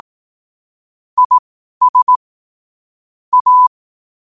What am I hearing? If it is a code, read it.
Morse code, heard as a keyed tone, about 18 wpm, spaced as for 12 wpm, 1000 Hz. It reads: IS A
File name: qotdfiles / 12wpm